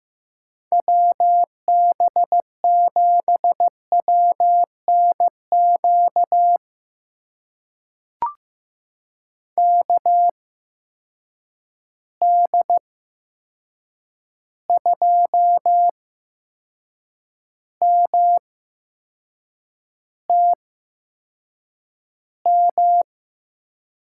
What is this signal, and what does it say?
Morse code, heard as a keyed tone, about 15 wpm, spaced as for 6 wpm, 700 Hz. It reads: WB7WNQ KD2MTM
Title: USA Call Signs - 6 Characters - 8x Character Spacing - 15wpm